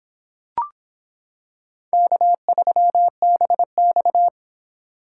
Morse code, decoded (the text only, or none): K3BX